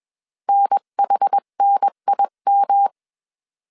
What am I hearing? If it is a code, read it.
Morse code, heard as a keyed tone, about 21 wpm, 800 Hz. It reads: NHNIM
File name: 20wpm